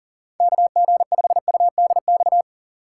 Morse code, heard as a keyed tone, about 40 wpm, 700 Hz. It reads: KG5UDX
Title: USA Call Signs - 3x Character Spacing - 40wpm